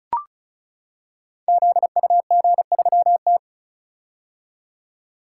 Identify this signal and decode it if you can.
Morse code, heard as a keyed tone, about 36 wpm, 700 Hz. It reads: ZUG3T
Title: Code Groups - 5 Letter:Numbers - 35wpm